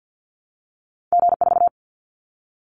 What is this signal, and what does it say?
Morse code, heard as a keyed tone, about 50 wpm, 700 Hz. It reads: G4
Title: Callsign Prefixes for Top 25 DXCC - 50wpm